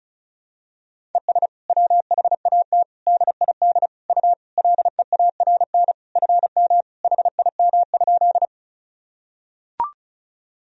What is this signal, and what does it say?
Morse code, heard as a keyed tone, about 36 wpm, 700 Hz. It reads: ES WHAT DID U LEARN FM HIM?
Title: Sentences from Top 300 Words and Abbreviations 35wpm